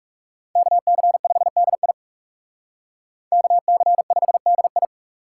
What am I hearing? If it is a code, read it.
Morse code, heard as a keyed tone, about 38 wpm, 700 Hz. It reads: KC5DI KC5DI